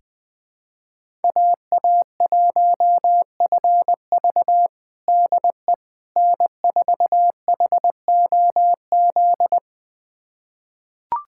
Morse code, heard as a keyed tone, about 20 wpm, 700 Hz. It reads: AA1FV DE N4HOZ
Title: USA Callsign DE Callsign - 5 Characters - 20wpm